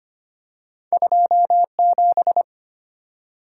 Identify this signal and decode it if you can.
Morse code, heard as a keyed tone, about 25 wpm, 700 Hz. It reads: 27
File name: Two Letter-Number Combinations - 25wpm